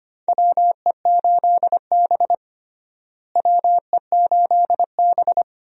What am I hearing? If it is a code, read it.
Morse code, heard as a keyed tone, about 25 wpm, 700 Hz. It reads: WE8B WE8B
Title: USA Call Signs - 4 Characters - Repeated 2x - 25wpm